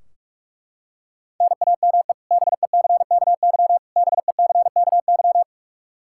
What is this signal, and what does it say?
Morse code, heard as a keyed tone, about 41 wpm, 700 Hz. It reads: NAME BECKY BECKY
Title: K1USN SST QSO Practice 45wpm